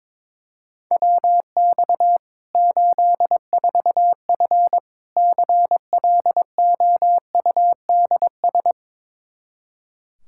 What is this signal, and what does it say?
Morse code, heard as a keyed tone, about 22 wpm, 700 Hz. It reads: WX 84F CLOUDS
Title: Practice QSO - Exchange 2 - Info - 22wpm